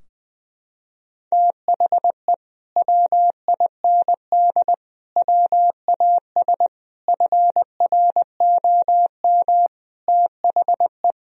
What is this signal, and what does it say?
Morse code, heard as a keyed tone, about 21 wpm, 700 Hz. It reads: THE WIND WAS FROM THE
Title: Sentences from Top 1000 Words - Review Part 2 - 20wpm